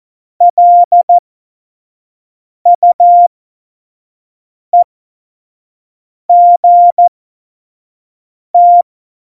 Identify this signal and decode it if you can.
Morse code, heard as a keyed tone, about 14 wpm, spaced as for 5 wpm, 700 Hz. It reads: LUEGT
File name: Morsekurs / Witze / Witz 23. Juli 2019, 25 F70